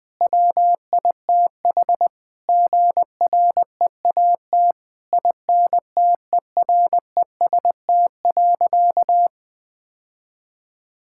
Morse code, heard as a keyed tone, about 20 wpm, 700 Hz. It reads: WITH GREAT INTEREST.
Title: Sentences from Top 1000 Words - Review Part 6 - 20wpm